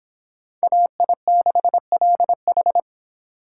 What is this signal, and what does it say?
Morse code, heard as a keyed tone, about 31 wpm, 700 Hz. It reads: AI6LH